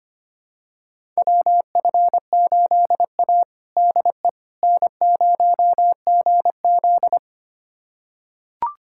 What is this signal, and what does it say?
Morse code, heard as a keyed tone, about 25 wpm, 700 Hz. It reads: WF8A DE N0GZ
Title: USA Callsign DE Callsign - 4 Characters - 25wpm